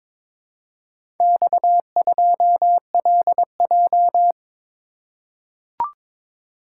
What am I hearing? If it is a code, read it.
Morse code, heard as a keyed tone, about 22 wpm, 700 Hz. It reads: X2LJ
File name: practice / 22wpm / Four Letter-Number Combinations - 22wpm